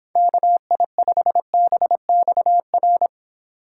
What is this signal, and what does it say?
Morse code, heard as a keyed tone, about 26 wpm, 700 Hz. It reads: KI5BXR